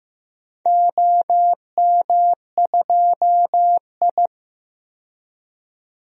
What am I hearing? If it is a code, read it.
Morse code, heard as a keyed tone, about 15 wpm, 700 Hz. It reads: OM2I
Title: DXCC-100 Call Signs - 4 Characters - 15wpm